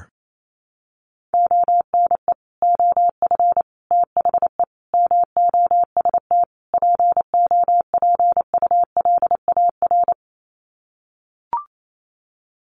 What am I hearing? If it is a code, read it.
Morse code, heard as a keyed tone, about 28 wpm, 700 Hz. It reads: ONE OF THE MOST POPULAR
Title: Sets of 5 Words 28wpm